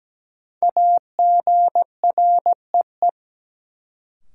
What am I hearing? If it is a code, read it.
Morse code, heard as a keyed tone, about 17 wpm, 700 Hz. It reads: AGREE
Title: Sets of 5 Words Encore - New Word Drill - 17wpm